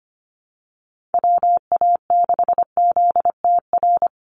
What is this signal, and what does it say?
Morse code, heard as a keyed tone, about 25 wpm, 700 Hz. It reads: WA6ZTR
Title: USA Call Signs 25wpm